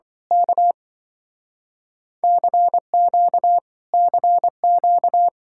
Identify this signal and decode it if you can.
Morse code, heard as a keyed tone, about 24 wpm, 700 Hz. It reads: K CQ CQ
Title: Speed-Racing Common QSO Phrases - 20wpm